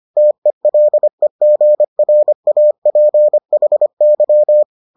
Morse code, heard as a keyed tone, about 25 wpm, 600 Hz. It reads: TELEGRAPHY